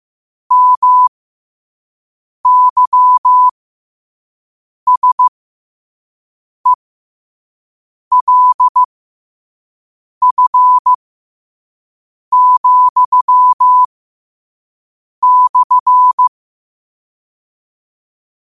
Morse code, heard as a keyed tone, about 15 wpm, spaced as for 5 wpm, 1000 Hz. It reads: MYSELF,/
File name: qotdfiles / 5wpm